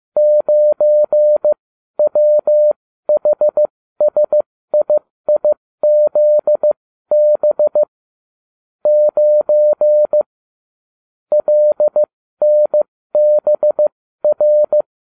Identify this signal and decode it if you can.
Morse code, heard as a keyed tone, about 15 wpm, spaced as for 13 wpm, 600 Hz. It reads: 9WHSIIZB 9 LNBR